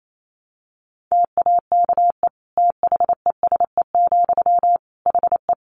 Morse code, heard as a keyed tone, about 28 wpm, 700 Hz. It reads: TAKE THESE, HE
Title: Sentences from Top 300 Words 28wpm